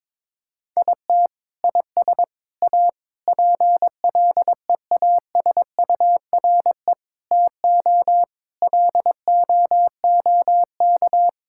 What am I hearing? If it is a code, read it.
Morse code, heard as a keyed tone, about 22 wpm, 700 Hz. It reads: IT IS A PLEASURE TO LOOK
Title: Sentences from Top 1600 Words 22wpm